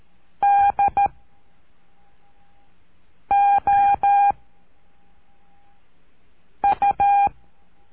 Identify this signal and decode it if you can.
Morse code, heard as a keyed tone, about 13 wpm, 800 Hz. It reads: D O U
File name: cursos / CURSO DE TELEGRAFIA / Telegrafia - Trilha 03